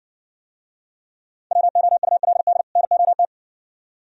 Morse code, heard as a keyed tone, about 59 wpm, 690 Hz. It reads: WOULD NOT